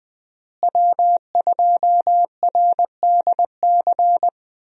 Morse code, heard as a keyed tone, about 20 wpm, 700 Hz. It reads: W2RDC
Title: USA Call Signs - 5 Characters - Repeated 3x - 20wpm